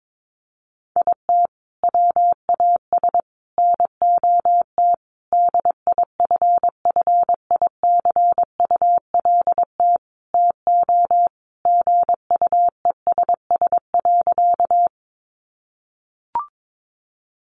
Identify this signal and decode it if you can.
Morse code, heard as a keyed tone, about 22 wpm, 700 Hz. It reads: IT WAS NOT DIFFICULT TO GUESS.
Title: Sentences from Top 1500 Words 22wpm